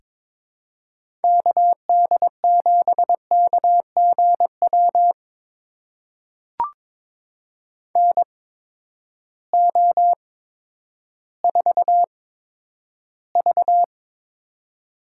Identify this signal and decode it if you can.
Morse code, heard as a keyed tone, about 23 wpm, spaced as for 11 wpm, 700 Hz. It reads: KD7KGW NO4V
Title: USA Call Signs - 8x Character Spacing - 22wpm